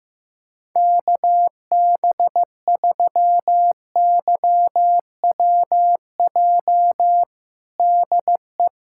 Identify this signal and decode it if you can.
Morse code, heard as a keyed tone, about 14 wpm, 700 Hz. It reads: KB3YWJ DE
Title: USA Callsign DE Callsign 15wpm